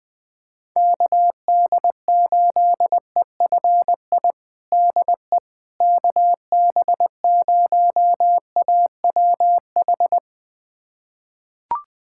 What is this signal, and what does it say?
Morse code, heard as a keyed tone, about 20 wpm, 700 Hz. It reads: KD8EFI DE KB0AWH
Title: USA Callsign DE Callsign - 6 Characters - 20wpm